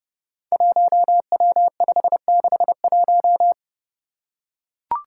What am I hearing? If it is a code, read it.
Morse code, heard as a keyed tone, about 32 wpm, 700 Hz. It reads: 1W561
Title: Five Challenging Letter-Number Combinations 30wpm